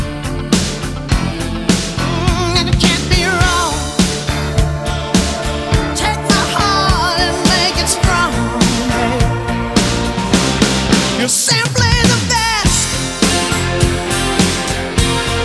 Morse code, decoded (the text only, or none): none